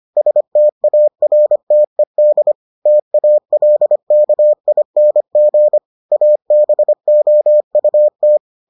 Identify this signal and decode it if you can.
Morse code, heard as a keyed tone, about 25 wpm, 600 Hz. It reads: STARTED TALKING ABOUT